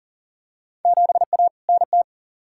Morse code, heard as a keyed tone, about 37 wpm, 700 Hz. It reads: 7A NT
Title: Field Day Exchanges Encore - 40wpm